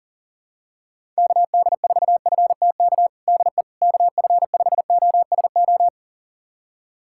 Morse code, heard as a keyed tone, about 40 wpm, 700 Hz. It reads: KD4FTK DE KF5OSO
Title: USA Callsign DE Callsign 40wpm